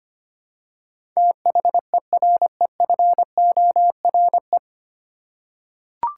Morse code, heard as a keyed tone, about 25 wpm, 700 Hz. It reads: THEREFORE
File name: practice / 25wpm / 9 Letter Words - First 3 Letters Given - 25wpm